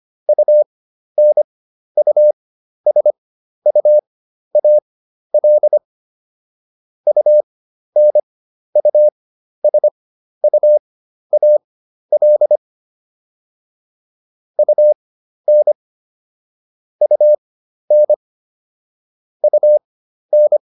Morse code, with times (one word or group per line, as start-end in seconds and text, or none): 0.3-5.8 UNUSUAL
7.1-12.6 UNUSUAL
14.6-15.7 UN
17.0-18.1 UN
19.4-20.6 UN